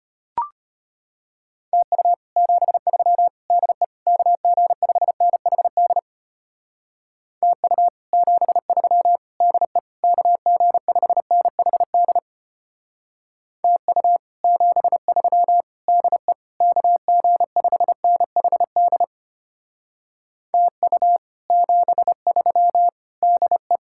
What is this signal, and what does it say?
Morse code, heard as a keyed tone, about 32 wpm, 700 Hz. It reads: TU 73 DE KG5NHD TU 73 DE KG5NHD TU 73 DE KG5NHD TU 73 DE